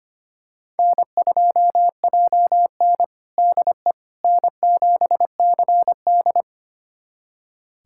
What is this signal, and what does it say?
Morse code, heard as a keyed tone, about 25 wpm, 700 Hz. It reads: N2JN DE N7CD
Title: USA Callsign DE Callsign - 4 Characters - 25wpm